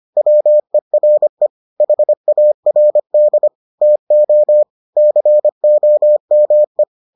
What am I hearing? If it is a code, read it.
Morse code, heard as a keyed tone, about 25 wpm, 600 Hz. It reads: WERE HARD TO COME